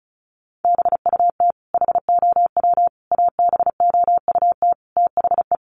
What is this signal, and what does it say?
Morse code, heard as a keyed tone, about 34 wpm, 700 Hz. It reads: BUT HOW ABOUT THE
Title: Sentences from Top 1400 Words 35wpm